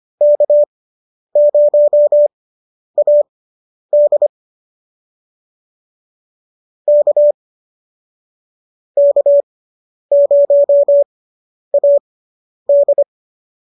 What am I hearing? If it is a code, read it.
Morse code, heard as a keyed tone, about 25 wpm, spaced as for 10 wpm, 590 Hz. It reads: K0AD K K0AD